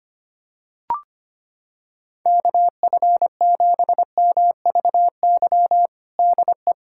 Morse code, heard as a keyed tone, about 25 wpm, 700 Hz. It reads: KF7MVY DE
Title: USA Callsign DE Callsign - 6 Characters - 25wpm